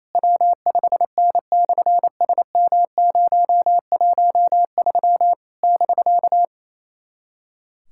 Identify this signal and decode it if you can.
Morse code, heard as a keyed tone, about 28 wpm, 700 Hz. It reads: W5N/SM013 <BK>
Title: SOTA QSO Practice 28wpm